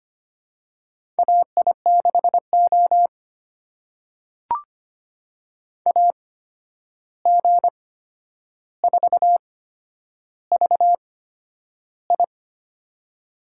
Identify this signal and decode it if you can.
Morse code, heard as a keyed tone, about 25 wpm, spaced as for 10 wpm, 700 Hz. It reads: AI6O AG4VI
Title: USA Call Signs - 8x Character Spacing - 25wpm